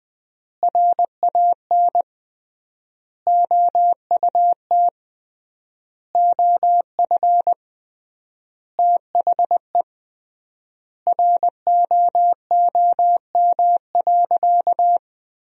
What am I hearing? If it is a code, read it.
Morse code, heard as a keyed tone, about 21 wpm, 700 Hz. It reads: RAN OUT OF THE ROOM.
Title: 3x Word-Spacing - Sentences from Top 500 Words - 20wpm